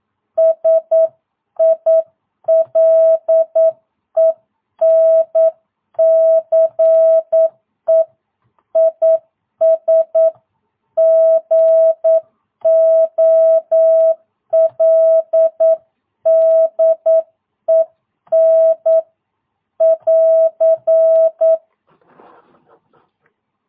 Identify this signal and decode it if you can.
Morse code, heard as a keyed tone, about 9 wpm, 660 Hz. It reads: SILENCE IS GOLDEN <AR>
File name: jh8cbh / 2022cw